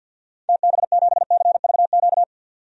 Morse code, heard as a keyed tone, about 52 wpm, 700 Hz. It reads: TB7CVQ
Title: DXCC-100 Call Signs - 6 Characters - 50wpm